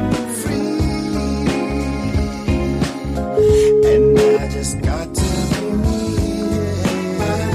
Ringback tone (Europe, 425 Hz), with noise.